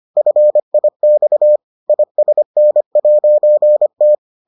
Morse code, heard as a keyed tone, about 25 wpm, 600 Hz. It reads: FIX ISN'T